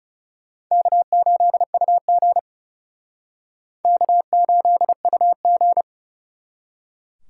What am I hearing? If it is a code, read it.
Morse code, heard as a keyed tone, about 37 wpm, 700 Hz. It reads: K8UG K8UG
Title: Speed Racing - USA 1x2 Call Signs 30wpm